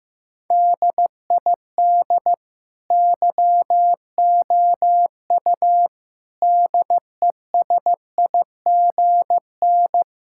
Morse code, heard as a keyed tone, about 15 wpm, 700 Hz. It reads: DID YOU DESIGN
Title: Sentences from Top 1000 Words - Review Part 5 - 15wpm